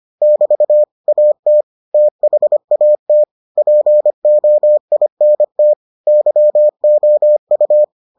Morse code, heard as a keyed tone, about 25 wpm, 600 Hz. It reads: <BT> AT THAT POINT YOU